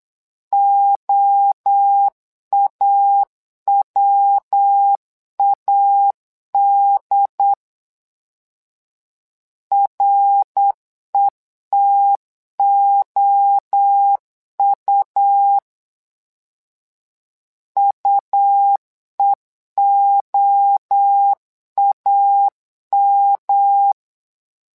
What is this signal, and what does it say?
Morse code, heard as a keyed tone, about 8 wpm, 800 Hz. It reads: OAWAD RETOU UEOAM
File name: LEZIONI / 3 MWRO / 3 MWRO 8wpm 2606